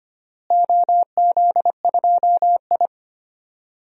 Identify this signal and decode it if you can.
Morse code, heard as a keyed tone, about 25 wpm, 700 Hz. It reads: OZ2I
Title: DXCC-100 Call Signs - 4 Characters - Repeated 3x - 25wpm